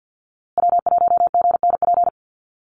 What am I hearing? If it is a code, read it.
Morse code, heard as a keyed tone, about 50 wpm, 700 Hz. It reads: W1GNP
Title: USA Call Signs Repeated 3x 50wpm